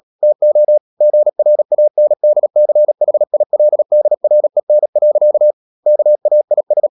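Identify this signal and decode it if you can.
Morse code, heard as a keyed tone, about 37 wpm, 600 Hz. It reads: TO GRANDCHILDREN. KAIS
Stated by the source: Morse code practice at 35–39 wpm sending English text of Ken Follett's novel Never, from CW med Gnister.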